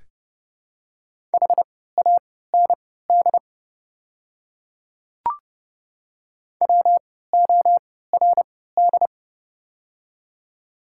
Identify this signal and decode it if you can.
Morse code, heard as a keyed tone, about 30 wpm, spaced as for 17 wpm, 690 Hz. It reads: HAND WORD
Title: 3x Character Spacing - Top 200 Words - 30wpm